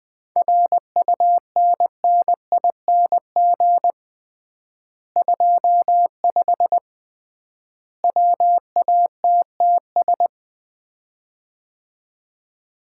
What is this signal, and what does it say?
Morse code, heard as a keyed tone, about 21 wpm, 700 Hz. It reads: RUNNING 25 WATTS
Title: Common QSO Phrases - 3x Word-Spacing - 20wpm